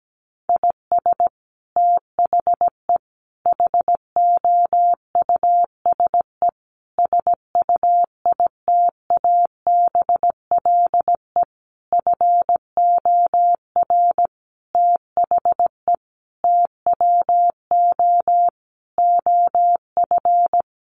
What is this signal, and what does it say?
Morse code, heard as a keyed tone, about 17 wpm, 700 Hz. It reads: IS THE HOUSE SUITABLE FOR THE TWO OF
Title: Sentences from Top 1400 Words 17wpm